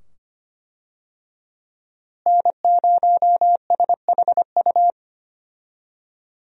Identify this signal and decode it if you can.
Morse code, heard as a keyed tone, about 25 wpm, 700 Hz. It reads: N0SHU
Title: USA Call Signs - 5 Characters - Repeated 2x - 25wpm